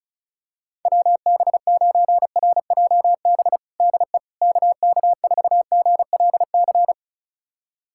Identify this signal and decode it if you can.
Morse code, heard as a keyed tone, about 34 wpm, 700 Hz. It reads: WB9RJB DE KK4GLC
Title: USA Callsign DE Callsign 35wpm